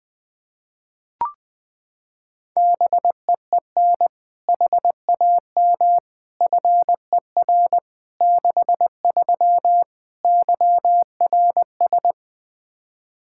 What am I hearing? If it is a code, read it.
Morse code, heard as a keyed tone, about 21 wpm, 700 Hz. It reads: BEEN HAM FER 63 YRS
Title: Practice QSO - Exchange 3 - More Info - 20wpm